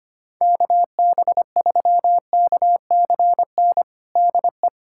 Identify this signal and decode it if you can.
Morse code, heard as a keyed tone, about 25 wpm, 700 Hz. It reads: KB3KCN DE
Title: USA Callsign DE Callsign - 6 Characters - 25wpm